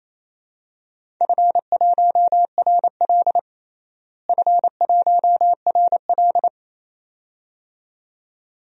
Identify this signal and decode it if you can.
Morse code, heard as a keyed tone, about 28 wpm, 700 Hz. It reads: F1RL F1RL
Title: DXCC-100 Call Signs - 4 Characters - Repeated 2x - 28wpm